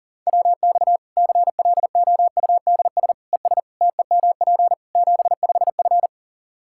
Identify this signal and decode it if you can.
Morse code, heard as a keyed tone, about 40 wpm, 700 Hz. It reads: WX CLOUDS ES TEMP 75F
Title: Practice QSO - Exchange 2 - Info - 40wpm